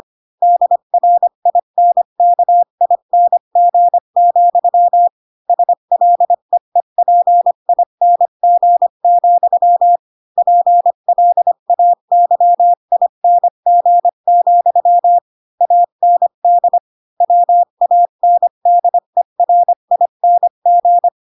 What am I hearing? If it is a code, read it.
Morse code, heard as a keyed tone, about 25 wpm, 700 Hz. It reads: DRINKING, SLEEPING, PLAYING, AND WANDERING
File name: Pinocchio / Pinocchio ch4 20 25 ik1hge